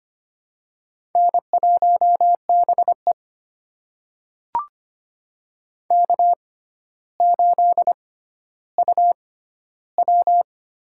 Morse code, heard as a keyed tone, about 25 wpm, spaced as for 13 wpm, 700 Hz. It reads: N1BE K8UW
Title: USA Call Signs - 4 Characters - 6x Character Spacing - 25wpm